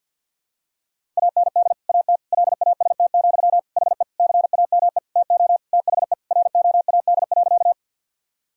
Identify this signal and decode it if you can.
Morse code, heard as a keyed tone, about 51 wpm, 700 Hz. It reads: AND AT LAST, HE CAME TO THE ROAD.
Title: Sentences from Top 1000 Words - Review Part 2 - 50wpm